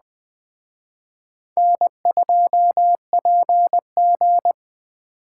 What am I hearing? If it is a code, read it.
Morse code, heard as a keyed tone, about 20 wpm, 700 Hz. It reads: N2PG
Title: Speed Racing - USA Call Signs 17wpm